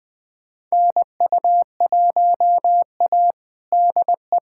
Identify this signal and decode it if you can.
Morse code, heard as a keyed tone, about 20 wpm, 700 Hz. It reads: NU1A DE
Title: USA Callsign DE Callsign - 4 Characters - 20wpm